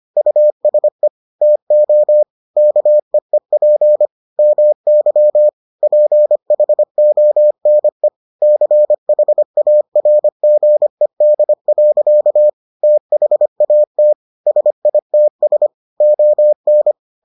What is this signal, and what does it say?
Morse code, heard as a keyed tone, about 25 wpm, 600 Hz. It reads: USE TO KEEP MY PHONE CHARGED. THAT SITS ON